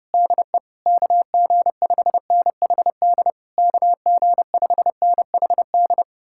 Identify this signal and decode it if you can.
Morse code, heard as a keyed tone, about 33 wpm, 700 Hz. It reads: DE KG5NHD KG5NHD